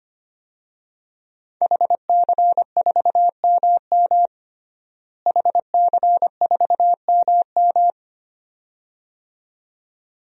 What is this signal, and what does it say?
Morse code, heard as a keyed tone, about 25 wpm, 700 Hz. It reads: HC4MM HC4MM
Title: DXCC-100 Call Signs - 5 Characters - Repeated 2x - 25wpm